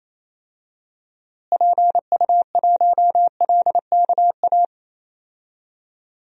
Morse code, heard as a keyed tone, about 28 wpm, 700 Hz. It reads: PU1LKA
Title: DXCC-100 Call Signs - 6 Characters - Repeated 3x - 28wpm